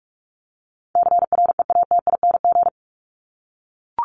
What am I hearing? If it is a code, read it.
Morse code, heard as a keyed tone, about 44 wpm, 700 Hz. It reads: CREATING
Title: ING Suffix - Top 100 Words with Base Words 45wpm